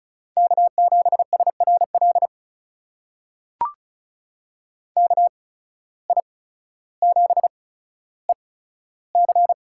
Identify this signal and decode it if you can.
Morse code, heard as a keyed tone, about 34 wpm, spaced as for 15 wpm, 700 Hz. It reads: K7SRL KI7EC
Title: USA Call Signs - 5 Characters - 8x Character Spacing - 35wpm